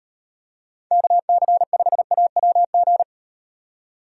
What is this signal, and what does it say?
Morse code, heard as a keyed tone, about 38 wpm, 700 Hz. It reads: KC5AWG